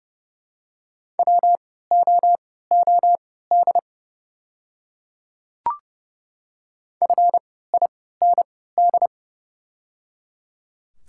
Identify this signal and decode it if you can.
Morse code, heard as a keyed tone, about 29 wpm, spaced as for 17 wpm, 700 Hz. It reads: WOOD FIND
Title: 3x Character Spacing - Top 300 Words - 30wpm